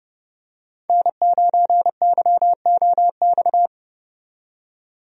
Morse code, heard as a keyed tone, about 32 wpm, 700 Hz. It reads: N9YOX
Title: USA Call Signs - 5 Characters - 30wpm